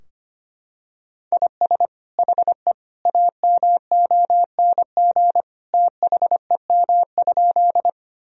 Morse code, heard as a keyed tone, about 25 wpm, 700 Hz. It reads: IS HE AMONG THEM?
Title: Sentences from Top 1000 Words - Review Part 3 - 25wpm